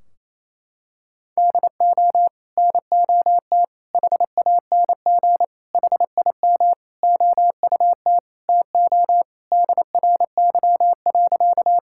Morse code, heard as a keyed tone, about 28 wpm, 700 Hz. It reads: DO NOT HANG HIM OUT TO DRY.
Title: Sentences from Top 1600 Words 28wpm